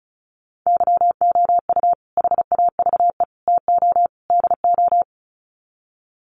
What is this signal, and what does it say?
Morse code, heard as a keyed tone, about 34 wpm, 700 Hz. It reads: YOU HAVE TO DO
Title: Sets of 4 Words 35wpm